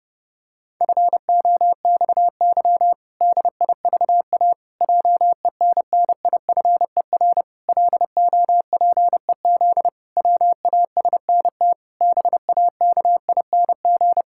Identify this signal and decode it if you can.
Morse code, heard as a keyed tone, about 32 wpm, 700 Hz. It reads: FOXY DIVA JENNIFER LOPEZ WASNT BAKING
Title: Pangram Challenge v2 - 30wpm